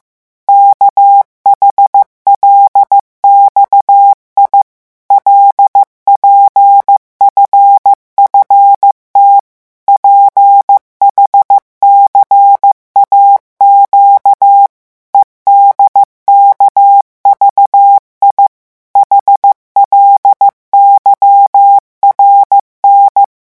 Morse code, heard as a keyed tone, about 15 wpm, 800 Hz. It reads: KHLXI LPFFT PHCAQ EDKVI HLYRN
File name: morse mp3 / revision7-vitesse 14 mots